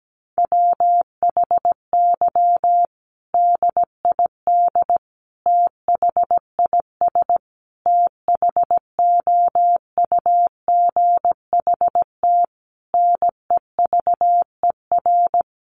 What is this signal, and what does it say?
Morse code, heard as a keyed tone, about 17 wpm, 700 Hz. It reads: WHY DID THIS THOUGHT NEVER
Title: Sentences from Top 900 Words 17wpm